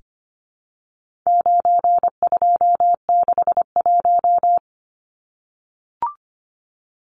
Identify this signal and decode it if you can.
Morse code, heard as a keyed tone, about 25 wpm, 700 Hz. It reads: 9261
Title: Numbers - 4-Digit 25wpm